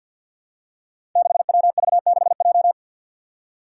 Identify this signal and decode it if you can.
Morse code, heard as a keyed tone, about 46 wpm, 700 Hz. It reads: 6WV6J
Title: Five Challenging Letter-Number Combinations 50wpm